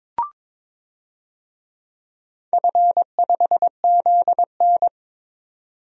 Morse code, heard as a keyed tone, about 23 wpm, 700 Hz. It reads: F5ZN
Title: DXCC-100 Call Signs - 4 Characters - 22wpm